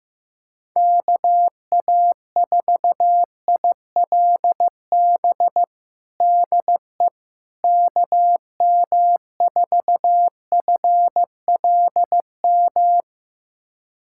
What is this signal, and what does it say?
Morse code, heard as a keyed tone, about 15 wpm, 700 Hz. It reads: KA4ILB DE KM4FLM
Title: USA Callsign DE Callsign 15wpm